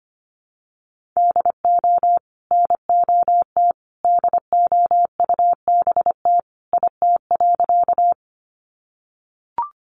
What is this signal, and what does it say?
Morse code, heard as a keyed tone, about 25 wpm, 700 Hz. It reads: DO NOT DOUBT IT.
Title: Sentences from Top 1500 Words 25wpm